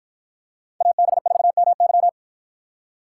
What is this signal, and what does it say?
Morse code, heard as a keyed tone, about 53 wpm, 700 Hz. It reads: AB4MY